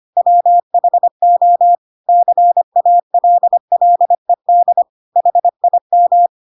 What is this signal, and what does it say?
Morse code, heard as a keyed tone, about 25 wpm, 700 Hz. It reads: WHO CALLED HIM
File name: Pinocchio / Pinocchio ch2 25 25 ik1hge